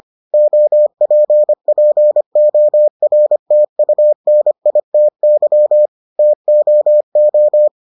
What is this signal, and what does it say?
Morse code, heard as a keyed tone, about 25 wpm, 600 Hz. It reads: OPPORTUNITY TOO